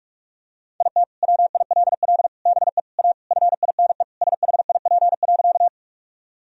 Morse code, heard as a keyed tone, about 46 wpm, 700 Hz. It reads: IT WILL BE A FINE SHIP.
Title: Sentences from Top 1000 Words - Review Part 2 - 45wpm